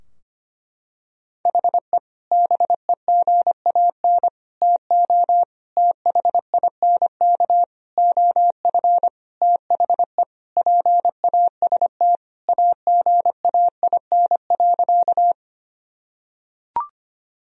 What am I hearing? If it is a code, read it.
Morse code, heard as a keyed tone, about 25 wpm, 700 Hz. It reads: HE BEGAN TO THINK OF THE PAST AGAIN.
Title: Sentences from Top 1000 Words - Review Part 3 - 25wpm